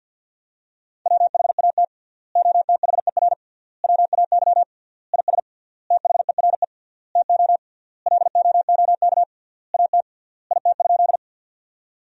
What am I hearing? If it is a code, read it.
Morse code, heard as a keyed tone, about 50 wpm, 700 Hz. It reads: WHAT OTHER WAY IS THERE TO LOOK AT IT?
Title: 3x Word-Spacing - Sentences from Top 100 Words - 50wpm